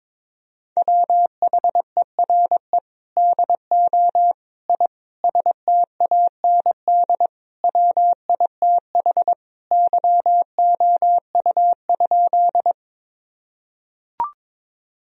Morse code, heard as a keyed tone, about 22 wpm, 700 Hz. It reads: WHERE DO I STAND WITH YOU?
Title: Sentences from Top 1000 Words - Review Part 1 - 22wpm